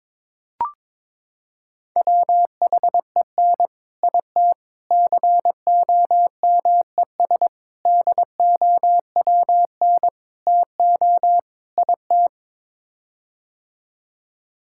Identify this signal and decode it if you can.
Morse code, heard as a keyed tone, about 22 wpm, 700 Hz. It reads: WHEN IT COMES DOWN TO IT
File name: practice / 22wpm / Sets of 6 Words - 22wpm